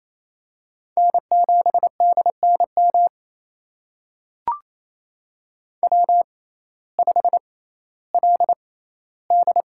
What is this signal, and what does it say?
Morse code, heard as a keyed tone, about 28 wpm, spaced as for 15 wpm, 700 Hz. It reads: N7DNM W5LD
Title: USA Call Signs - 6x Character Spacing - 28wpm